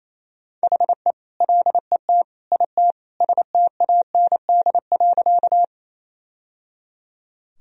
Morse code, heard as a keyed tone, about 28 wpm, 700 Hz. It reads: HE LET IT STAND.
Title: Sentences from Top 1000 Words - Review Part 1 - 28wpm